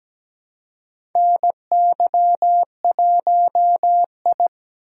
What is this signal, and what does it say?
Morse code, heard as a keyed tone, about 17 wpm, 700 Hz. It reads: NY1I
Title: USA Call Signs - 4 Characters - 2x Character Spacing - 17wpm